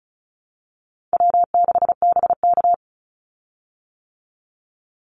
Morse code, heard as a keyed tone, about 34 wpm, 700 Hz. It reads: W6BK
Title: Call Signs - 35wpm